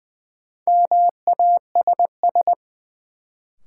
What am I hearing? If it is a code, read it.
Morse code, heard as a keyed tone, about 20 wpm, 700 Hz. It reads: MASS
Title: Sets of 5 Words Encore - New Word Drill - 20wpm